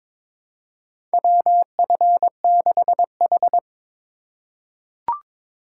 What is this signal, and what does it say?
Morse code, heard as a keyed tone, about 21 wpm, 700 Hz. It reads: WF6H
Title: USA Call Signs - 4 Characters - 3x Character Spacing - 22wpm